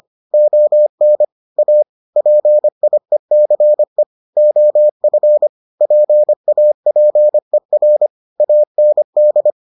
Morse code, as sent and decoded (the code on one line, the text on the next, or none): --- -. / .- / .--. .. -.-. / --- ..-. / .--. .- .--. .-. / .- -. -..
ON A PIECE OF PAPER AND